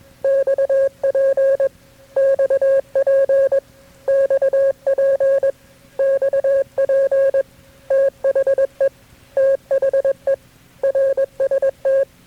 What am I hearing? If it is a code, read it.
Morse code, heard as a keyed tone, about 21 wpm, 540 Hz. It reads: XP XP XP XP THE THE RST
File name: KB6MT-tape3-side1